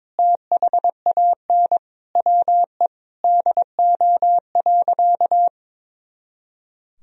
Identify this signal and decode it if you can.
Morse code, heard as a keyed tone, about 22 wpm, 700 Hz. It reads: THAN WE DO.
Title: Sentences from Top 100 Words and Abbreviations 22wpm